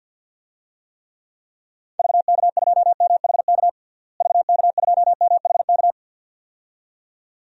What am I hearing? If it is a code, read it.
Morse code, heard as a keyed tone, about 51 wpm, 700 Hz. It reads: VK2MHK VK2MHK
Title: DXCC-100 Call Signs - 6 Characters - Repeated 2x - 50wpm